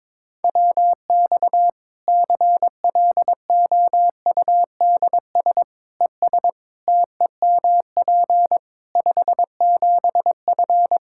Morse code, heard as a keyed tone, about 22 wpm, 700 Hz. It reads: WX CLOUDS ES TEMP 57F
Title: Practice QSO - Exchange 2 - Info - 22wpm